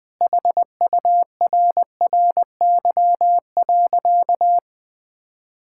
Morse code, heard as a keyed tone, about 20 wpm, 700 Hz. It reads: HURRY.
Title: Sentences with 5-Word Phrases - Spoken Beginning - 20wpm